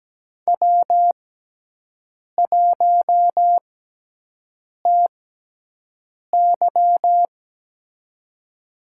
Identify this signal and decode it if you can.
Morse code, heard as a keyed tone, about 17 wpm, spaced as for 6 wpm, 700 Hz. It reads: W1TY
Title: USA Call Signs - 4 Characters - 6x Character Spacing - 17wpm